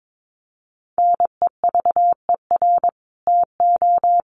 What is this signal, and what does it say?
Morse code, heard as a keyed tone, about 22 wpm, 700 Hz. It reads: NEVER TO
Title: Sentences from Top 800 Words 22wpm